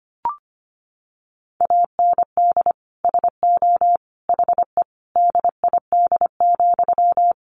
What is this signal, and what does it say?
Morse code, heard as a keyed tone, about 25 wpm, 700 Hz. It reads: AND SO HE DID,
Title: Sentences from Top 1500 Words 25wpm